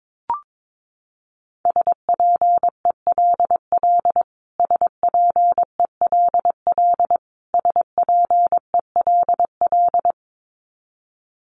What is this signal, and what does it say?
Morse code, heard as a keyed tone, about 22 wpm, 700 Hz. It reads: SPELL SPELL SPELL
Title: Top 200 Words Repeated 22wpm